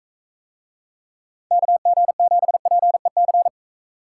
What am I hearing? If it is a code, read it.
Morse code, heard as a keyed tone, about 42 wpm, 700 Hz. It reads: KG7PEC